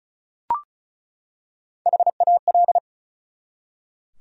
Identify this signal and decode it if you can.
Morse code, heard as a keyed tone, about 36 wpm, 700 Hz. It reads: HAL
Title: USA Common Names - Max 4 Letters - 35wpm